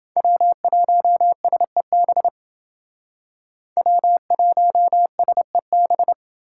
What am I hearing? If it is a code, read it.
Morse code, heard as a keyed tone, about 28 wpm, 700 Hz. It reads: W1SEB W1SEB